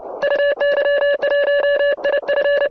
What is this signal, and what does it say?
Morse code, heard as a keyed tone, about 30 wpm, 1700 Hz, with noise. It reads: UY1IF